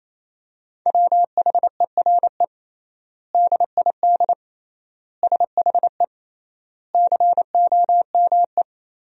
Morse code, heard as a keyed tone, about 27 wpm, 700 Hz. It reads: WHERE DID SHE COME
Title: 3x Word-Spacing - Sentences from Top 200 Words - 28wpm